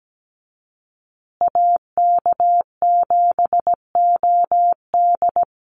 Morse code, heard as a keyed tone, about 17 wpm, 700 Hz. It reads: AK7OD
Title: USA Call Signs Repeated 3x 17wpm